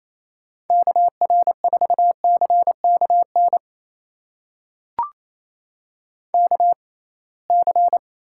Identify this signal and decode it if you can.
Morse code, heard as a keyed tone, about 28 wpm, spaced as for 21 wpm, 700 Hz. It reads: KR4CKN KC
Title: USA Call Signs - 6 Characters - 6x Character Spacing - 28wpm